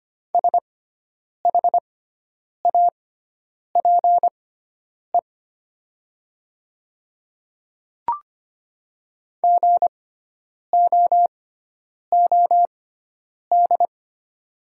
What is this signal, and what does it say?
Morse code, heard as a keyed tone, about 25 wpm, spaced as for 9 wpm, 700 Hz. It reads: SHAPE GOOD
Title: 6x Character Spacing - Top 500 Words - 25wpm